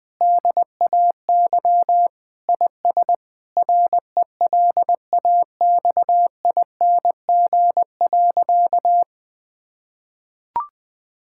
Morse code, heard as a keyed tone, about 20 wpm, 700 Hz. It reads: DAY IS RELAXING.
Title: Sentences with 3-Word Phrases - 20wpm